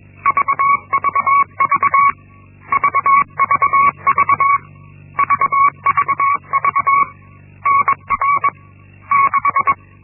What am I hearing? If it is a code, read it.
Morse code, heard as a keyed tone, about 21 wpm, 1120 Hz. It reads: VVV VVV VVV NR 6
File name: Linksmakalnis-1